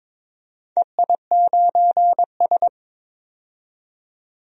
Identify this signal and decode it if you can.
Morse code, heard as a keyed tone, about 22 wpm, 700 Hz. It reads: EI9S
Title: DXCC-100 Call Signs - 4 Characters - 22wpm